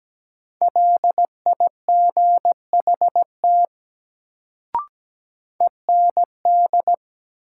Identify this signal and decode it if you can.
Morse code, heard as a keyed tone, about 17 wpm, 700 Hz. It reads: LIGHT END
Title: Sentences from Top 200 Words - New Word Drill - 17wpm